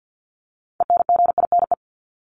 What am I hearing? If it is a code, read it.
Morse code, heard as a keyed tone, about 50 wpm, 700 Hz. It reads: ENGINE
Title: Top 1251-1500 Words 50wpm